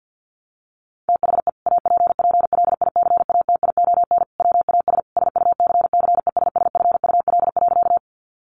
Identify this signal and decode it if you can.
Morse code, heard as a keyed tone, about 50 wpm, 700 Hz. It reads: THE APPLICATION WAS SUCCESSFUL.
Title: Sentences from Top 1200 Words 50wpm